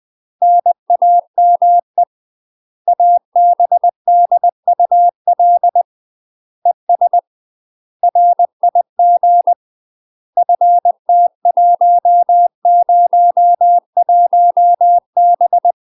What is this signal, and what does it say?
Morse code, heard as a keyed tone, about 20 wpm, 700 Hz. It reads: NAME ABDUL ES RIG FT101B
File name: UK MASTER QSOs TRAINING / 20WPM / UK MASTER QSOs 0034